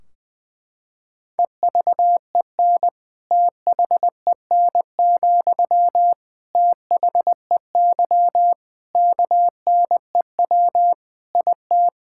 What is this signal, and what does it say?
Morse code, heard as a keyed tone, about 19 wpm, 700 Hz. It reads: EVEN THEN, THEY KNEW IT
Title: Sentences from Top 1600 Words 20wpm